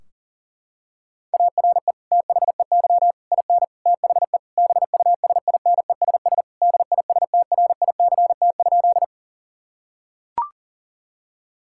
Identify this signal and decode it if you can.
Morse code, heard as a keyed tone, about 40 wpm, 700 Hz. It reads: ARE THEY IN THE BUSINESS DISTRICT?
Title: Sentences from Top 1600 Words 40wpm